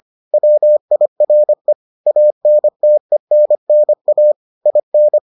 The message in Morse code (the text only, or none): WIRE ANTENNA IN